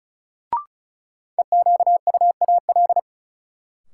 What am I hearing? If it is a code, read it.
Morse code, heard as a keyed tone, about 36 wpm, 700 Hz. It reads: EQUAL
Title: Sentences from Top 1400 Words - New Word Drill Alt - 35wpm